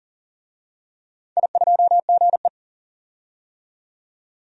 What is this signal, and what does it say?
Morse code, heard as a keyed tone, about 40 wpm, 700 Hz. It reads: I2GE
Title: DXCC-100 Call Signs - 4 Characters - 40wpm